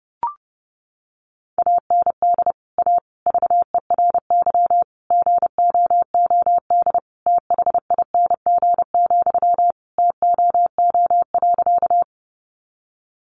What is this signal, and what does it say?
Morse code, heard as a keyed tone, about 30 wpm, 700 Hz. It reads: AND A VERY GOOD THING, TOO.
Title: Sentences from Top 200 Words 30wpm